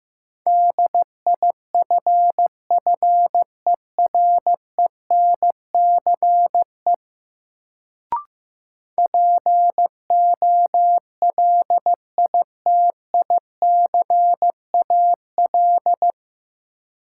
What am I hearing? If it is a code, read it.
Morse code, heard as a keyed tone, about 15 wpm, 700 Hz. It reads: DIFFERENCE POLITICAL
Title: Sentences from Top 800 Words - New Word Drill - 15wpm